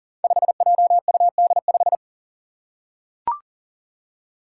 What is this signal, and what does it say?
Morse code, heard as a keyed tone, about 40 wpm, 700 Hz. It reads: 5JUD5
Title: Five Challenging Letter-Number Combinations 40wpm